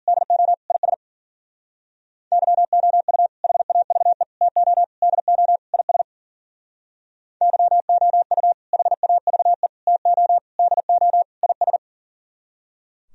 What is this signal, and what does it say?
Morse code, heard as a keyed tone, about 51 wpm, 700 Hz. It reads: DO IS YOU HAVE TO DO IS YOU HAVE TO DO IS